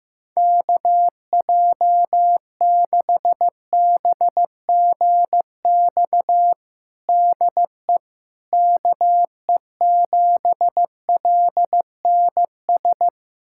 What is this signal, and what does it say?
Morse code, heard as a keyed tone, about 15 wpm, 700 Hz. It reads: KJ6BGX DE KE7LNS
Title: USA Callsign DE Callsign - 6 Characters - 15wpm